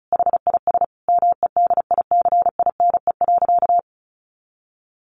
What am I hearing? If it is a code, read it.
Morse code, heard as a keyed tone, about 34 wpm, 700 Hz. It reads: HIS MEDICINE.
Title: Sentences from Top 900 Words 35wpm